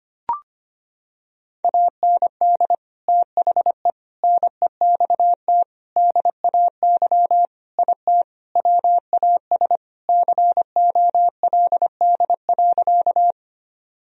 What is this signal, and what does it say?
Morse code, heard as a keyed tone, about 25 wpm, 700 Hz. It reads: AND THE NEXT DAY IT WAS COLD.
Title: Sentences from Top 1000 Words - Review Part 1 - 25wpm